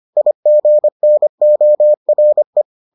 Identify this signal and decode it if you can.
Morse code, heard as a keyed tone, about 25 wpm, 600 Hz. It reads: IGNORE